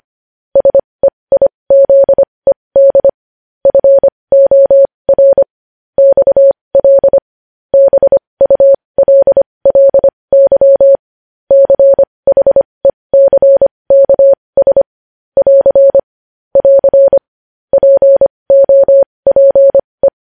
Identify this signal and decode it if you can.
Morse code, heard as a keyed tone, about 25 wpm, spaced as for 20 wpm, 550 Hz. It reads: SEIZED FOR XL BULLY CHECKS <AR> <AR> POPE
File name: podcast / News headlines-1x-25x20